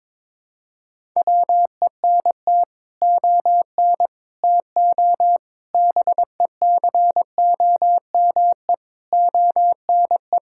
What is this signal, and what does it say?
Morse code, heard as a keyed tone, about 22 wpm, 700 Hz. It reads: WENT ON TO BECOME ONE
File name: practice / 22wpm / Sets of 7 Words - 22wpm